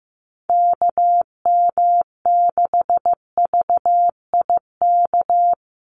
Morse code, heard as a keyed tone, about 15 wpm, 700 Hz. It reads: KM6VIK
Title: USA Call Signs 15wpm